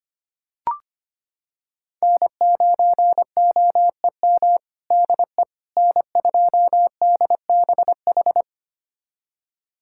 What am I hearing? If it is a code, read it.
Morse code, heard as a keyed tone, about 25 wpm, 700 Hz. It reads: N9OEM DE N2DBH
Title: USA Callsign DE Callsign - 5 Characters - 25wpm